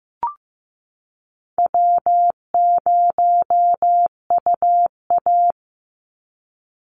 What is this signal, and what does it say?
Morse code, heard as a keyed tone, about 15 wpm, 700 Hz. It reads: W0UA
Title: Easy Call Signs - 15wpm